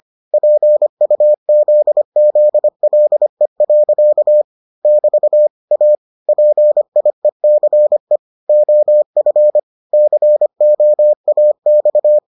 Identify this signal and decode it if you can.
Morse code, heard as a keyed tone, about 25 wpm, 600 Hz. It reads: PUZZLE. <BT> A PIECE OF COAX